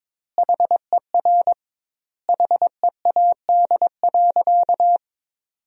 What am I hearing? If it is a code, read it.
Morse code, heard as a keyed tone, about 22 wpm, 700 Hz. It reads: HER HEAD.
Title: 2x Word-Spacing - Sentences from Top 300 Words - 22wpm